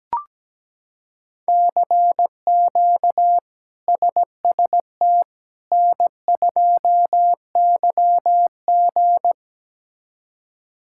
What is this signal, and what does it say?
Morse code, heard as a keyed tone, about 17 wpm, 700 Hz. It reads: CQ SST N2YG
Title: K1USN SST QSO Practice - 17wpm